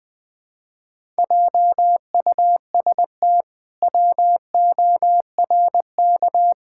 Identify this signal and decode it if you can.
Morse code, heard as a keyed tone, about 20 wpm, 700 Hz. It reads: JUST WORK
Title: Pangram Challenge v2 - 20wpm